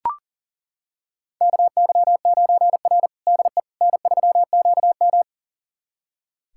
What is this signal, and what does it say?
Morse code, heard as a keyed tone, about 40 wpm, 700 Hz. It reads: KY9R DE N3QM
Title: USA Callsign DE Callsign - 4 Characters - 40wpm